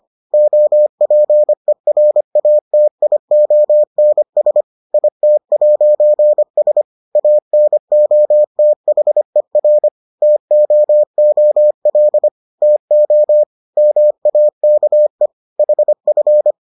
Morse code, heard as a keyed tone, about 25 wpm, 600 Hz. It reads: OPERATIONS IT'S ANOTHER TOOL TO MAKE HF